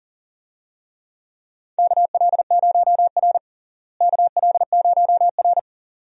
Morse code, heard as a keyed tone, about 40 wpm, 700 Hz. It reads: KL0R KL0R
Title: USA Call Signs - 4 Characters - Repeated 2x - 40wpm